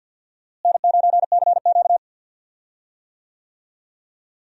Morse code, heard as a keyed tone, about 51 wpm, 700 Hz. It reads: N9CQ
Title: USA Call Signs - 4 Characters - Repeated 2x - 50wpm